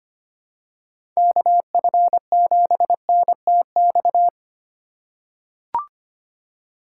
Morse code, heard as a keyed tone, about 25 wpm, 700 Hz. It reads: KF7NTX
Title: USA Call Signs - 6 Characters - 2x Character Spacing - 25wpm